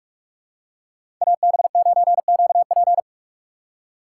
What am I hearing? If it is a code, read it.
Morse code, heard as a keyed tone, about 46 wpm, 700 Hz. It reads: AB9QP
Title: USA Call Signs - 5 Characters - 4x Character Spacing - 45wpm